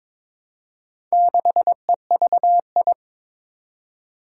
Morse code, heard as a keyed tone, about 23 wpm, 700 Hz. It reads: 6EVI
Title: Four Challenging Letter-Number Combinations 22wpm